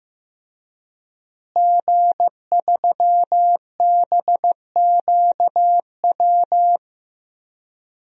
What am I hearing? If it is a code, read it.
Morse code, heard as a keyed tone, about 15 wpm, 700 Hz. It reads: G3BQW
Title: DXCC-100 Call Signs - 5 Characters - Repeated 2x - 15wpm